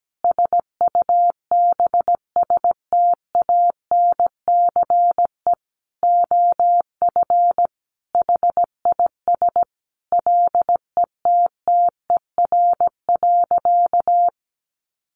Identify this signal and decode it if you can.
Morse code, heard as a keyed tone, about 17 wpm, 700 Hz. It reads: SUBSTANCE OF HIS LETTER.
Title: Sentences from Top 1500 Words 17wpm